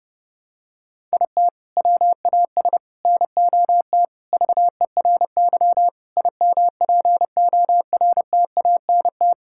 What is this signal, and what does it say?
Morse code, heard as a keyed tone, about 30 wpm, 700 Hz. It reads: IT WAS NOT VERY IMPORTANT
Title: Sentences from Top 1000 Words - Review Part 5 - 30wpm